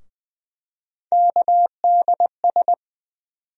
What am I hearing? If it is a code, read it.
Morse code, heard as a keyed tone, about 20 wpm, 700 Hz. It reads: KDS
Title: USA Call Sign Postfixes - 3 Letters 20wpm